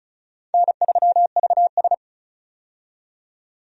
Morse code, heard as a keyed tone, about 36 wpm, 700 Hz. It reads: N3VS
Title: USA Call Signs - 4 Characters - 35wpm